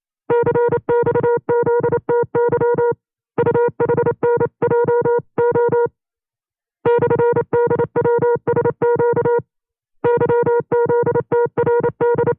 Morse code, heard as a keyed tone, about 28 wpm, 480 Hz. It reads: CXZTY UHNJO /DWSQ YZTRD
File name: 25WPM